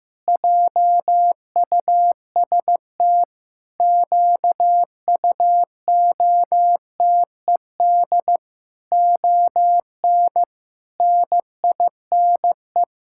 Morse code, heard as a keyed tone, about 15 wpm, 700 Hz. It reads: JUST QUOTED ON NINE
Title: Pangram Challenge v2 - 15wpm